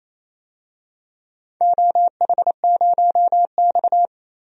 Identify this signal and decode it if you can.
Morse code, heard as a keyed tone, about 28 wpm, 700 Hz. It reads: OH0X
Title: DXCC-100 Call Signs - 4 Characters - Repeated 3x - 28wpm